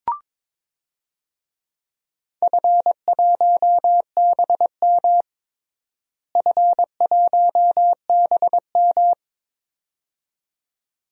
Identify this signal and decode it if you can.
Morse code, heard as a keyed tone, about 22 wpm, 700 Hz. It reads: F1BM F1BM
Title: DXCC-100 Call Signs - 4 Characters - Repeated 2x - 22wpm